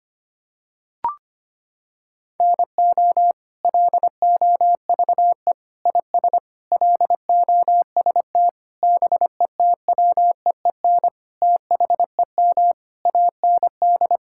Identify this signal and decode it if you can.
Morse code, heard as a keyed tone, about 25 wpm, 700 Hz. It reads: NO LOVE IS LOST BETWEEN THEM AND
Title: Sentences from Top 1600 Words 25wpm